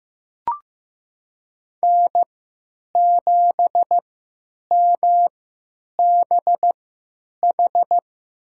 Morse code, heard as a keyed tone, about 15 wpm, spaced as for 9 wpm, 700 Hz. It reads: N7MBH
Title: USA Call Signs - 5 Characters - 3x Character Spacing - 15wpm